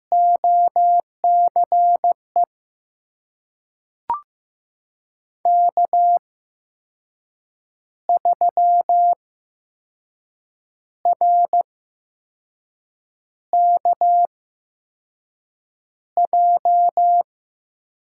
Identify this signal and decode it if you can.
Morse code, heard as a keyed tone, about 15 wpm, spaced as for 5 wpm, 700 Hz. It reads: OCE K3RKJ
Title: USA Call Signs - 5 Characters - 8x Character Spacing - 15wpm